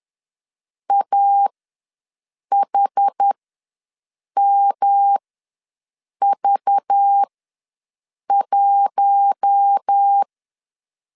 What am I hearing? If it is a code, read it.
Morse code, heard as a keyed tone, about 10 wpm, spaced as for 6 wpm, 800 Hz. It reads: AHMV1
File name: LEZIONI / 5 LBVC / 5 LBV ws 10wpm 3645